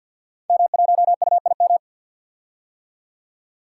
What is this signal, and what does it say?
Morse code, heard as a keyed tone, about 46 wpm, 700 Hz. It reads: M1UIM
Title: Code Groups - 5 Letter:Numbers - 50wpm